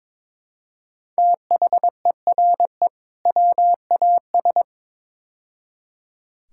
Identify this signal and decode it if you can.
Morse code, heard as a keyed tone, about 22 wpm, 700 Hz. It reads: THERE WAS
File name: practice / 22wpm / Build A Phrase - 5-Word Phrases - 22wpm